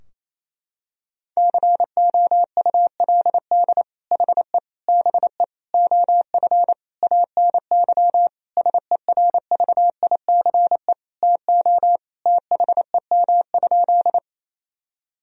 Morse code, heard as a keyed tone, about 28 wpm, 700 Hz. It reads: COULD HE BE OF ANY SERVICE TO THEM?
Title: Sentences from Top 1000 Words - Review Part 4 - 28wpm